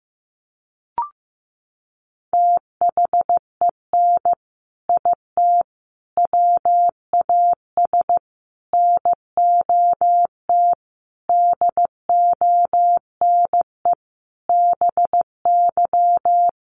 Morse code, heard as a keyed tone, about 15 wpm, 700 Hz. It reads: THEN IT WAS NOT DONE BY
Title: Sentences from Top 1300 Words 15wpm